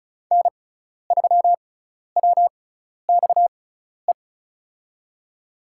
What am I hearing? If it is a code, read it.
Morse code, heard as a keyed tone, about 34 wpm, spaced as for 12 wpm, 700 Hz. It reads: N3WXE
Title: USA Call Signs - 5 Characters - 6x Character Spacing - 35wpm